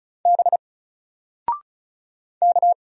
Morse code, heard as a keyed tone, about 36 wpm, 700 Hz. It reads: B K
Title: Single Letter-Number 35wpm